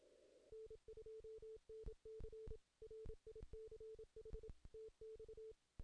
Morse code, heard as a keyed tone, about 27 wpm, 440 Hz. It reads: N2NC RICH TX
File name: 2016 JAN CW NAQP / 2016 Jan naqp k5ot - 0-10